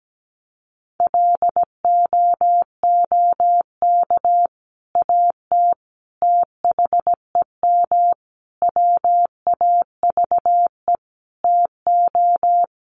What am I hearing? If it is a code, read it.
Morse code, heard as a keyed tone, about 17 wpm, 700 Hz. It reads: LOOK AT THEM WAVE TO
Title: Sentences from Top 1400 Words 17wpm